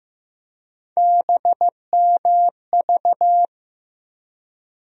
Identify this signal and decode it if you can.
Morse code, heard as a keyed tone, about 15 wpm, 700 Hz. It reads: BMV
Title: USA Call Sign Postfixes - 3 Letters 15wpm